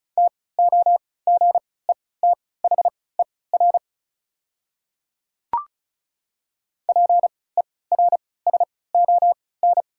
Morse code, heard as a keyed tone, about 36 wpm, spaced as for 20 wpm, 690 Hz. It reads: TOGETHER PERSON
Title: 3x Character Spacing - Top 500 Words - 35wpm